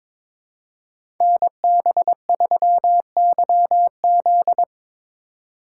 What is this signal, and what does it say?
Morse code, heard as a keyed tone, about 22 wpm, 700 Hz. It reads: NB3YZ
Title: Code Groups - 5 Letter:Numbers - 22wpm